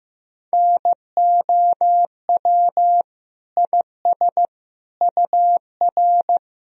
Morse code, heard as a keyed tone, about 15 wpm, 700 Hz. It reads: NOW IS UR
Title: Sentences from Top 100 Words and Abbreviations 15wpm